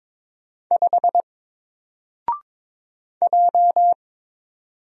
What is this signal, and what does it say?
Morse code, heard as a keyed tone, about 22 wpm, 700 Hz. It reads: 5 J
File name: Single-Letter-Number-22wpm